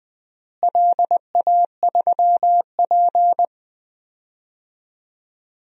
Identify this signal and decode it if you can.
Morse code, heard as a keyed tone, about 20 wpm, 700 Hz. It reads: LA3P
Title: DXCC-100 Call Signs - 4 Characters - Repeated 3x - 20wpm